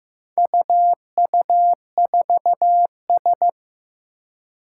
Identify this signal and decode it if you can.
Morse code, heard as a keyed tone, about 15 wpm, 700 Hz. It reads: UU4S
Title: Four Challenging Letter-Number Combinations 15wpm